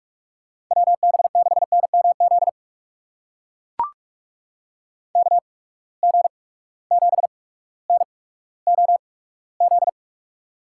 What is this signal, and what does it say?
Morse code, heard as a keyed tone, about 43 wpm, spaced as for 19 wpm, 700 Hz. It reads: WB6NMZ KG7NOZ
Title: USA Call Signs - 6 Characters - 8x Character Spacing - 45wpm